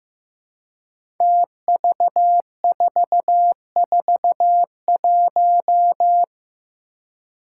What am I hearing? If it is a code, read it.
Morse code, heard as a keyed tone, about 15 wpm, 700 Hz. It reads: TV441
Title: Five Challenging Letter-Number Combinations 15wpm